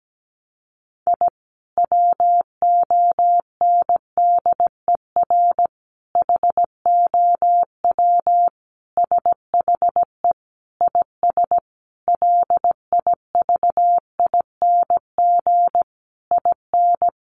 Morse code, heard as a keyed tone, about 17 wpm, 700 Hz. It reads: I WONDER HOW SHE IS LIVING IN
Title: Sentences from Top 1500 Words 17wpm